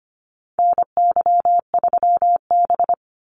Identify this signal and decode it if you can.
Morse code, heard as a keyed tone, about 25 wpm, 700 Hz. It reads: NY3B
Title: Call Signs - 25wpm